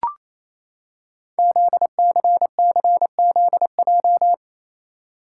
Morse code, heard as a keyed tone, about 27 wpm, 690 Hz. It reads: ZCCZJ
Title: Code Groups - 5 Letters - 28wpm